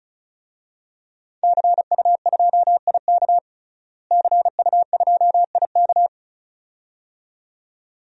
Morse code, heard as a keyed tone, about 34 wpm, 700 Hz. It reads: CU2IK CU2IK
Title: DXCC-100 Call Signs - 5 Characters - Repeated 2x - 35wpm